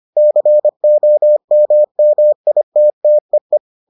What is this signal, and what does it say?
Morse code, heard as a keyed tone, about 25 wpm, 600 Hz. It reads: COMMITTEE